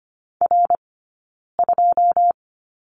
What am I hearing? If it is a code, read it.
Morse code, heard as a keyed tone, about 25 wpm, 700 Hz. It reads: R 2
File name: Lesson 15d - Number 5 - Two Characters 25wpm